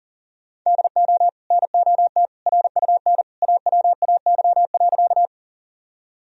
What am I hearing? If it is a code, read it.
Morse code, heard as a keyed tone, about 40 wpm, 700 Hz. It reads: DO NOT RUN AWAY.
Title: Sentences from Top 1000 Words - Review Part 1 - 40wpm